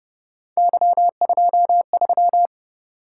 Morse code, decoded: Y23

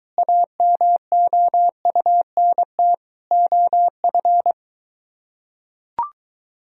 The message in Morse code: AMOUNT OF

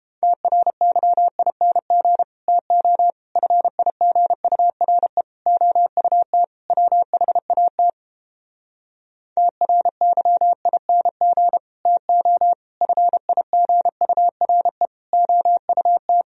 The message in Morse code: TRYING TO FIGURE OUT WHAT TRYING TO FIGURE OUT

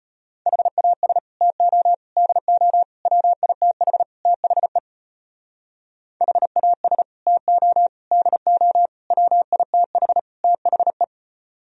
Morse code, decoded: HAS TO DO WITH THE HAS TO DO WITH THE

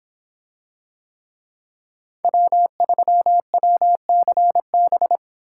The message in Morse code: W3WCB

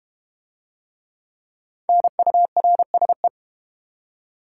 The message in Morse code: NURSE

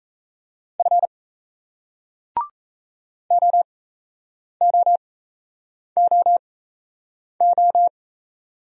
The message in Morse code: F OOOO

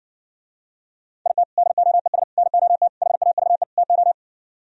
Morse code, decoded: IT DOES NOT HAVE TO

